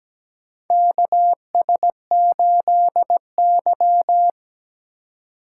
Morse code, heard as a keyed tone, about 17 wpm, 700 Hz. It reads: KS8Y
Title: USA Call Signs - 4 Characters - 2x Character Spacing - 17wpm